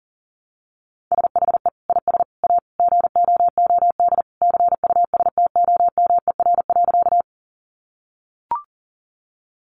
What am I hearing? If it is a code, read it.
Morse code, heard as a keyed tone, about 40 wpm, 700 Hz. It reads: SHE IS A GOOD CUSTOMER.
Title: Sentences from Top 1100 Words 40wpm